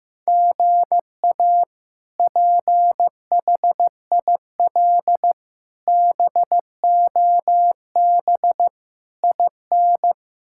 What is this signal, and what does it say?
Morse code, heard as a keyed tone, about 15 wpm, 700 Hz. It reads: GA PHIL BOB IN
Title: K1USN SST QSO Practice - 15wpm